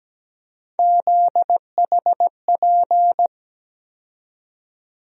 Morse code, heard as a keyed tone, about 17 wpm, 700 Hz. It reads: ZHP
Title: USA Call Sign Postfixes - 3 Letters 17wpm